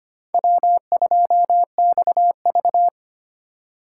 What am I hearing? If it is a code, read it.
Morse code, heard as a keyed tone, about 25 wpm, 700 Hz. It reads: W2XV
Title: USA Call Signs - 4 Characters - 4x Character Spacing - 25wpm